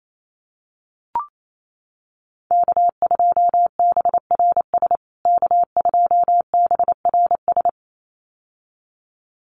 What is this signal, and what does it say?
Morse code, heard as a keyed tone, about 28 wpm, 700 Hz. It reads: K2BRS K2BRS
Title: USA Call Signs Repeated 2x 28wpm